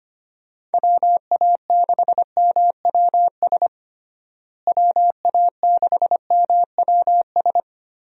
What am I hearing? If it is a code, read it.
Morse code, heard as a keyed tone, about 25 wpm, 700 Hz. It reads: WA6MWS WA6MWS
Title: USA Call Signs - 6 Characters - Repeated 3x - 25wpm